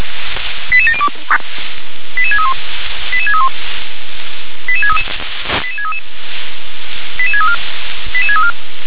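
Five-tone selective call (ZVEI), heard as quick runs of five tones, tones of about 70 ms. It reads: 88702 88631 88631 88633 88633 88634 88634
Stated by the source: ZVEI five-tone selective call 88702 received by SDR.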